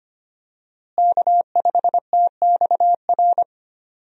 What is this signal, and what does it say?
Morse code, heard as a keyed tone, about 25 wpm, 700 Hz. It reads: K5TXR
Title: USA Call Signs - 5 Characters - Repeated 2x - 25wpm